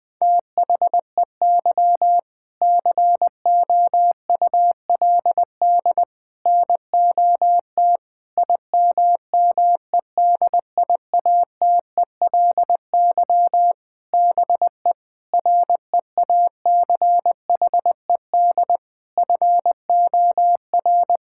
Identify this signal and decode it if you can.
Morse code, heard as a keyed tone, about 20 wpm, 700 Hz. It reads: THEY COULD NOT IMMEDIATELY BE REACHED FOR
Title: Sentences with 7-Word Phrases - 20wpm